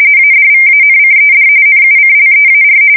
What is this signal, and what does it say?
RTTY, about 45.45 baud. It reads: ADVANCED NUCLEAR
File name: fsk170 / usnavy1